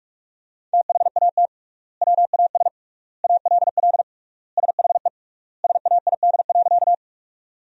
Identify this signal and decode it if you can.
Morse code, heard as a keyed tone, about 44 wpm, 700 Hz. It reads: THAT WAS ALL SHE SAID.
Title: 3x Word-Spacing - Sentences from Top 100 Words - 45wpm